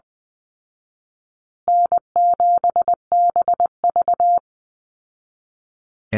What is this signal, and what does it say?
Morse code, heard as a keyed tone, about 20 wpm, 700 Hz. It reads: N7BV
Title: Call Signs Repeated 2x - 20wpm